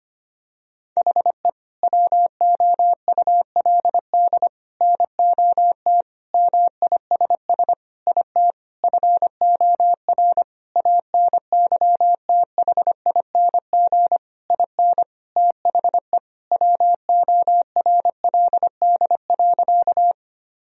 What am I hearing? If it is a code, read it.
Morse code, heard as a keyed tone, about 25 wpm, 700 Hz. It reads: HE WOULD NOT MISS IT FOR ANYTHING IN THE WORLD.
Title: Sentences from Top 1000 Words - Review Part 3 - 25wpm